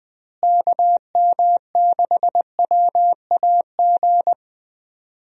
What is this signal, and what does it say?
Morse code, heard as a keyed tone, about 20 wpm, 700 Hz. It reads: KM6WAG
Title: USA Call Signs - 6 Characters - Repeated 3x - 20wpm